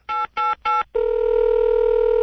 DTMF, heard as telephone touch tones, tones of about 180 ms. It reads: 888